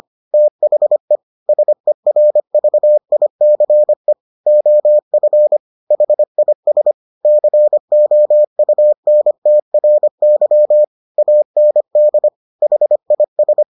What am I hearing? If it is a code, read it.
Morse code, heard as a keyed tone, about 25 wpm, 600 Hz. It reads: THE SERVICE OF HIS COUNTRY AND HIS